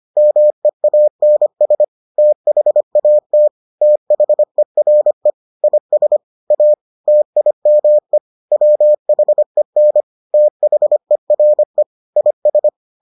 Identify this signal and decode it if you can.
Morse code, heard as a keyed tone, about 25 wpm, 600 Hz. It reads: MEANS THAT THERE IS A TIME WHEN THERE IS